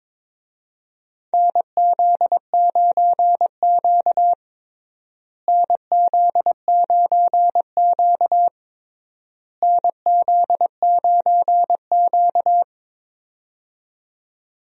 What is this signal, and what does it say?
Morse code, heard as a keyed tone, about 21 wpm, 700 Hz. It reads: NZ9Q NZ9Q NZ9Q
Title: USA Call Signs - 4 Characters - Repeated 3x - 22wpm